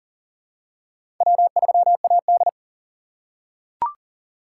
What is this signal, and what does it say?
Morse code, heard as a keyed tone, about 40 wpm, 700 Hz. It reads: W3AD
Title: USA Call Signs - 4 Characters - 3x Character Spacing - 40wpm